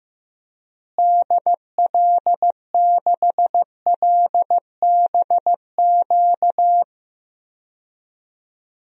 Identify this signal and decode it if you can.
Morse code, heard as a keyed tone, about 15 wpm, 700 Hz. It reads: DL6LBQ
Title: DXCC-100 Call Signs - 6 Characters - Repeated 3x - 15wpm